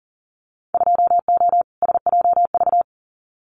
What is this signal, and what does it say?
Morse code, heard as a keyed tone, about 40 wpm, 700 Hz. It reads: 2O SJV
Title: Winter Field Day Exchanges Encore 40wpm